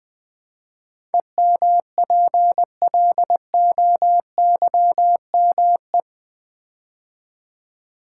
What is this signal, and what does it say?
Morse code, heard as a keyed tone, about 20 wpm, 700 Hz. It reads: EMPLOYME
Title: Build A Word - 4 Letters At A Time - 10-Letter Words - 20wpm